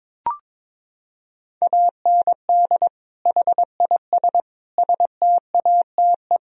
Morse code, heard as a keyed tone, about 21 wpm, 700 Hz. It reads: AND HIS STATE
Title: Sentences from Top 1000 Words - Review Part 2 - 22wpm